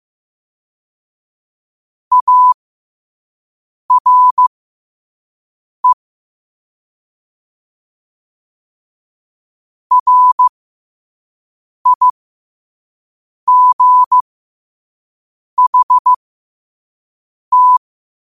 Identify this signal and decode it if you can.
Morse code, heard as a keyed tone, about 15 wpm, spaced as for 5 wpm, 1000 Hz. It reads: ARE RIGHT
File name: qotdfiles / 5wpm